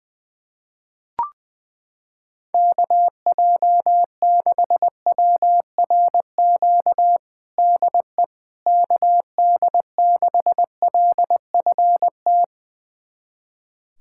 Morse code, heard as a keyed tone, about 20 wpm, 700 Hz. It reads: KJ6WRQ DE KD6LFT
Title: USA Callsign DE Callsign - 6 Characters - 20wpm